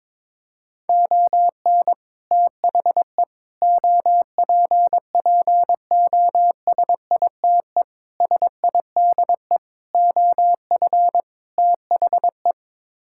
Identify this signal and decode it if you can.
Morse code, heard as a keyed tone, about 22 wpm, 700 Hz. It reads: ON THE OPPOSITE SIDE OF THE
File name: practice / 22wpm / Sets of 6 Words - 22wpm